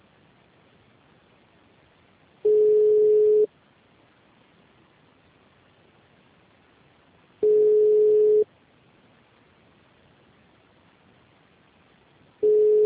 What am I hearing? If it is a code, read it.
Ringback tone (Europe, 425 Hz).